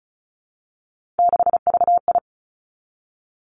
Morse code, heard as a keyed tone, about 36 wpm, 700 Hz. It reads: 6VI
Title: Three Challenging Letter-Number Combinations 35wpm